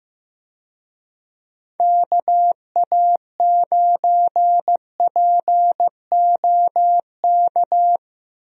Morse code, heard as a keyed tone, about 15 wpm, 700 Hz. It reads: KA9POK
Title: USA Call Signs - 6 Characters - Repeated 3x - 15wpm